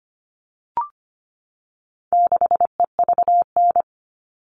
Morse code, heard as a keyed tone, about 25 wpm, 700 Hz. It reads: 6EVN
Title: All Cut Numbers - 4-Digit 25wpm